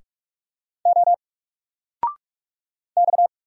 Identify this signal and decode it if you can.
Morse code, heard as a keyed tone, about 46 wpm, 700 Hz. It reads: O X
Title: Single Character (Letter:Number:Punctuation:Prosign) 45wpm